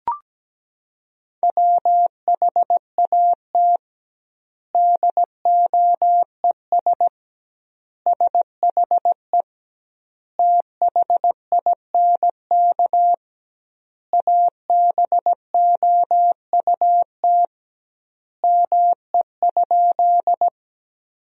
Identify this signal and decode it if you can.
Morse code, heard as a keyed tone, about 17 wpm, 700 Hz. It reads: WHAT DOES SHE THINK ABOUT ME?
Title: 2x Word-Spacing - Sentences from Top 200 Words - 17wpm